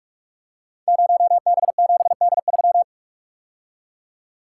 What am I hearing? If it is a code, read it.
Morse code, heard as a keyed tone, about 44 wpm, 690 Hz. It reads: 0B7D3
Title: Code Groups - 5 Letter:Numbers - 45wpm